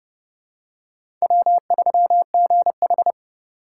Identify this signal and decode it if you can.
Morse code, heard as a keyed tone, about 31 wpm, 700 Hz. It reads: W3GH
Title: USA Call Signs - 2x Character Spacing - 30wpm